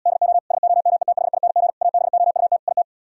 Morse code, heard as a keyed tone, about 77 wpm, 700 Hz. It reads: DO SOMETHING ABOUT IT